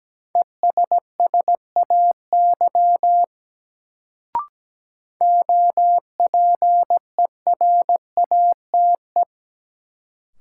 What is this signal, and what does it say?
Morse code, heard as a keyed tone, about 17 wpm, 700 Hz. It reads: ESSAY OPERATE
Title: Sentences from Top 1500 Words - New Word Drill Alt - 17wpm